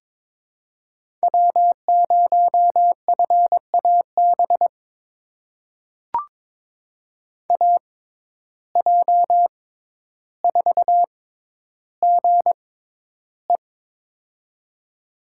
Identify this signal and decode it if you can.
Morse code, heard as a keyed tone, about 23 wpm, spaced as for 11 wpm, 700 Hz. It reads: W0FAB AJ4GE
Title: USA Call Signs - 5 Characters - 6x Character Spacing - 22wpm